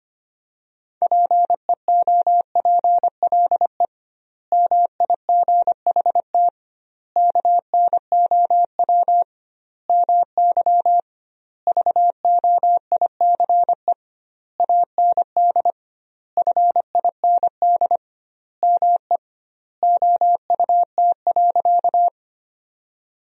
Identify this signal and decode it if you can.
Morse code, heard as a keyed tone, about 25 wpm, 700 Hz. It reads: PEOPLE MIGHT KNOW MY VOICE AND FIND ME OUT.
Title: 2x Word-Spacing - Sentences from Top 500 Words - 25wpm